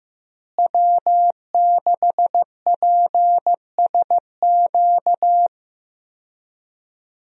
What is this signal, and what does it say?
Morse code, heard as a keyed tone, about 15 wpm, 700 Hz. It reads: W6PSQ
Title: USA Call Signs - 5 Characters - Repeated 3x - 15wpm